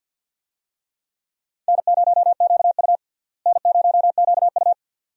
Morse code, heard as a keyed tone, about 47 wpm, 700 Hz. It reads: N0QU N0QU
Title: USA Call Signs - 4 Characters - Repeated 2x - 50wpm